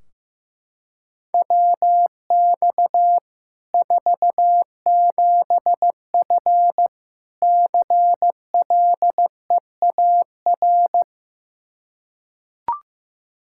Morse code, heard as a keyed tone, about 16 wpm, 700 Hz. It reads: WX 47F CLEAR
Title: Practice QSO - Exchange 2 - Info - 15wpm